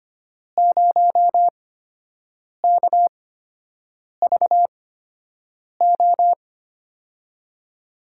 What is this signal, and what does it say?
Morse code, heard as a keyed tone, about 25 wpm, spaced as for 7 wpm, 700 Hz. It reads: 0KVO
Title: USA Call Signs - 6 Characters - 8x Character Spacing - 25wpm